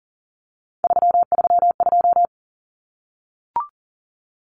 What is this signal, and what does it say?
Morse code, heard as a keyed tone, about 40 wpm, 700 Hz. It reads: 332
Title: Common Cut Numbers - 3-Digit 40wpm